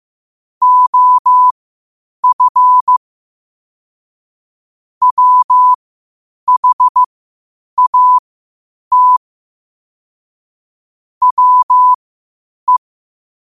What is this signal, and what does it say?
Morse code, heard as a keyed tone, about 15 wpm, spaced as for 8 wpm, 1000 Hz. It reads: OF WHAT WE
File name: qotdfiles / 8wpm